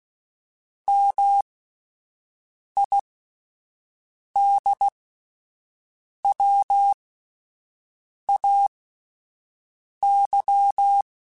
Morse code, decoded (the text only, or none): MIDWAY